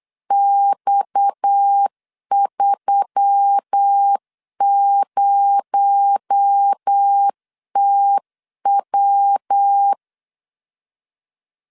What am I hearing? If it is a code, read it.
Morse code, heard as a keyed tone, about 8 wpm, 800 Hz. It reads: X30TW
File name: CW / DET / 1 6 1500C S C 8wpm 5219